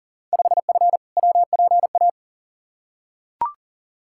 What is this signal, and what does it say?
Morse code, heard as a keyed tone, about 37 wpm, 700 Hz. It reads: 5F WPA
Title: Field Day Exchanges Encore - 40wpm